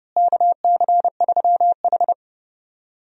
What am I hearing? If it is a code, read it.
Morse code, heard as a keyed tone, about 30 wpm, 700 Hz. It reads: KC3H
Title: USA Call Signs - 4 Characters - Repeated 2x - 30wpm